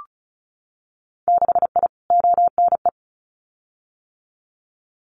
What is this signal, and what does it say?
Morse code, heard as a keyed tone, about 36 wpm, 700 Hz. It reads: 6I ONE
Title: Winter Field Day Exchanges Encore 35wpm